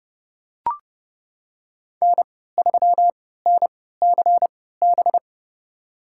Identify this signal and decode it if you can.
Morse code, heard as a keyed tone, about 29 wpm, spaced as for 17 wpm, 700 Hz. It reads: N3NCB
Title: USA Call Signs - 5 Characters - 3x Character Spacing - 30wpm